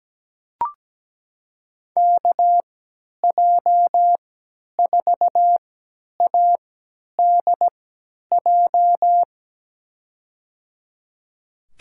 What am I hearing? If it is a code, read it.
Morse code, heard as a keyed tone, about 17 wpm, spaced as for 10 wpm, 700 Hz. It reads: KJ4ADJ
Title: USA Call Signs - 6 Characters - 3x Character Spacing - 17wpm